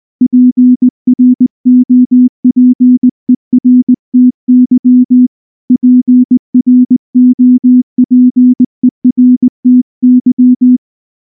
Morse code, heard as a keyed tone, about 19 wpm, 250 Hz. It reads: PROPERTY PROPERTY
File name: morse-training-20WPM